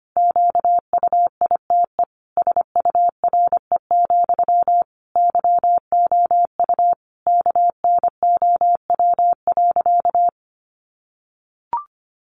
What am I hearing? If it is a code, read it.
Morse code, heard as a keyed tone, about 25 wpm, 700 Hz. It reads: QUITE SURE, YOU KNOW.
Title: Sentences from Top 700 Words 25wpm